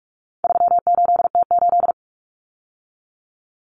Morse code, heard as a keyed tone, about 44 wpm, 700 Hz. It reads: 38T8
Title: Common Cut Numbers - 4-Digit 45wpm